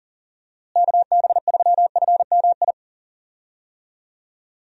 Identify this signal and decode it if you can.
Morse code, heard as a keyed tone, about 40 wpm, 700 Hz. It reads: KB3FMI
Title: USA Call Signs - 6 Characters - Repeated 2x - 40wpm